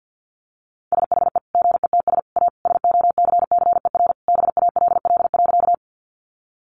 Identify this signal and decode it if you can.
Morse code, heard as a keyed tone, about 50 wpm, 700 Hz. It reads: SHE GETS A SOCCER BALL.
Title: Easy Phrases 50wpm